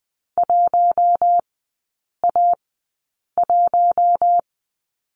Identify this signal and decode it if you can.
Morse code, heard as a keyed tone, about 20 wpm, spaced as for 8 wpm, 700 Hz. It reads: 1A1
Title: 041 Lesson 8e - Number 1 - Three Characters 20wpm